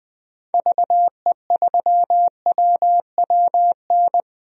Morse code, heard as a keyed tone, about 20 wpm, 700 Hz. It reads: VE3WWN
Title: DXCC-100 Call Signs - 6 Characters - 20wpm